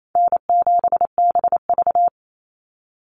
Morse code, heard as a keyed tone, about 28 wpm, 700 Hz. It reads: N7BV
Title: Easy Call Signs Repeated 3x - 28wpm